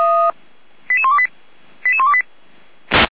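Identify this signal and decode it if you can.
Five-tone selective call (ZVEI), heard as quick runs of five tones, tones of about 70 ms. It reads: 88128 88128
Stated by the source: ZVEI five-tone call 88112 received by SDR.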